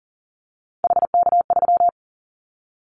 Morse code, heard as a keyed tone, about 40 wpm, 700 Hz. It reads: HK3
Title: Callsign Prefixes for Top 100 DXCC - 40wpm